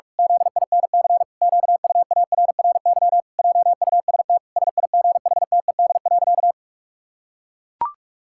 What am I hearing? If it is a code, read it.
Morse code, heard as a keyed tone, about 41 wpm, 700 Hz. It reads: ZINC QUARRY JUST SIGHTED.
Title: Pangram Challenge v2 - 45wpm